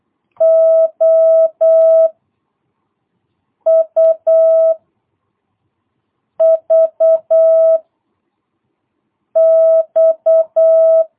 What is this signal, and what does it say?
Morse code, heard as a keyed tone, about 8 wpm, 660 Hz. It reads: O U V X
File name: jh8cbh / 2022cw